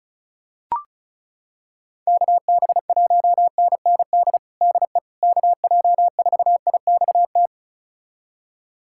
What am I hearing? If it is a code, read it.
Morse code, heard as a keyed tone, about 36 wpm, 700 Hz. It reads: KB1NND DE KJ4IXT
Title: USA Callsign DE Callsign - 6 Characters - 35wpm